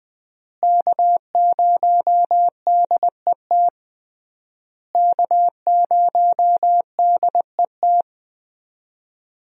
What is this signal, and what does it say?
Morse code, heard as a keyed tone, about 20 wpm, 700 Hz. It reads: K0DET K0DET
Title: USA Call Signs - 5 Characters - Repeated 3x - 20wpm